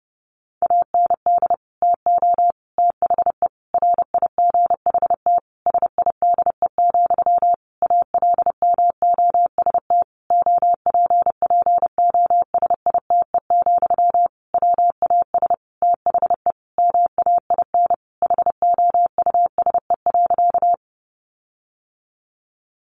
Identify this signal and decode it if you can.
Morse code, heard as a keyed tone, about 30 wpm, 700 Hz. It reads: AND TO THE RIGHT SIDE, ALMOST OPPOSITE, WAS THE MAIN HOUSE.
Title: Sentences from Top 1300 Words 30wpm